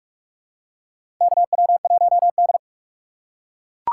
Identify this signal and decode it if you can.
Morse code, heard as a keyed tone, about 48 wpm, 700 Hz. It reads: KW1D
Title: USA Call Signs - 4x Character Spacing - 45wpm